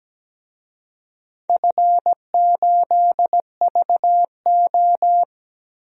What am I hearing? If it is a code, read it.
Morse code, heard as a keyed tone, about 17 wpm, 700 Hz. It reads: F8VO
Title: DXCC-100 Call Signs - 4 Characters - Repeated 3x - 17wpm